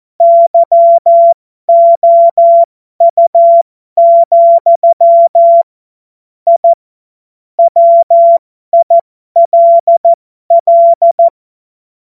Morse code, heard as a keyed tone, about 14 wpm, spaced as for 12 wpm, 680 Hz. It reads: YOU, I WILL
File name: Morsekurs / Witze